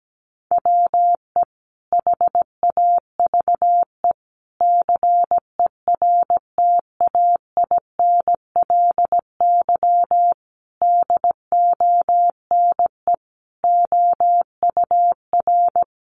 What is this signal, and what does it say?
Morse code, heard as a keyed tone, about 17 wpm, 700 Hz. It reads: WE HAVE CERTAINLY DONE OUR
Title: Sentences from Top 900 Words 17wpm